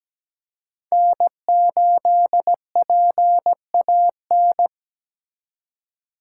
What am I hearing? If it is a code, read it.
Morse code, heard as a keyed tone, about 17 wpm, 700 Hz. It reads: N8PAN
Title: USA Call Signs - 5 Characters - 4x Character Spacing - 17wpm